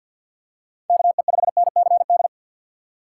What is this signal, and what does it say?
Morse code, heard as a keyed tone, about 52 wpm, 700 Hz. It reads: KE5NCD
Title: USA Call Signs - 6 Characters - 6x Character Spacing - 50wpm